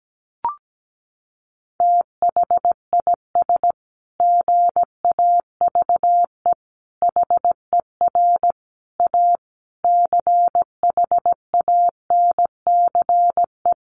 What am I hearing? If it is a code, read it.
Morse code, heard as a keyed tone, about 17 wpm, 700 Hz. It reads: THIS GAVE HER A CHANCE
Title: Sentences from Top 1400 Words 17wpm